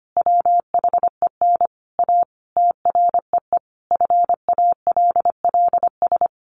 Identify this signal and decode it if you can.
Morse code, heard as a keyed tone, about 25 wpm, 700 Hz. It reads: WHEN A TREE FALLS
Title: A Sound Of Thunder 25wpm